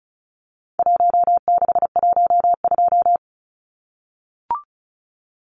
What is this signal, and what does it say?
Morse code, heard as a keyed tone, about 34 wpm, 700 Hz. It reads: 1612
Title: Numbers - Contest Style - 4-Digit 35wpm